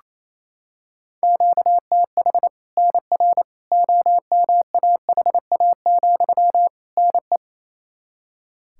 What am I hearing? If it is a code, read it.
Morse code, heard as a keyed tone, about 28 wpm, 700 Hz. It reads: QTH NR OMAHA, NE
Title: Practice QSO - Exchange 1 - Beginning - 28wpm